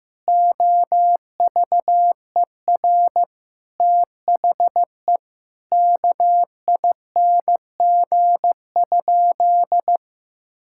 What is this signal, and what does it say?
Morse code, heard as a keyed tone, about 15 wpm, 700 Hz. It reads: OVER THE KING?
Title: Sentences from Top 1000 Words - Review Part 6 - 15wpm